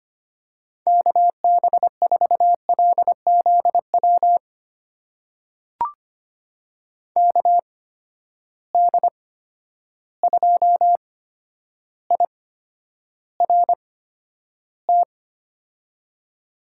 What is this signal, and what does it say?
Morse code, heard as a keyed tone, about 25 wpm, spaced as for 11 wpm, 700 Hz. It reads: KB4LZW KD2IRT
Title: USA Call Signs - 6 Characters - 8x Character Spacing - 25wpm